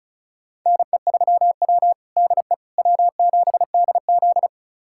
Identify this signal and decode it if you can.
Morse code, heard as a keyed tone, about 36 wpm, 700 Hz. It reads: NE3W DE W7DZ
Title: USA Callsign DE Callsign - 4 Characters - 35wpm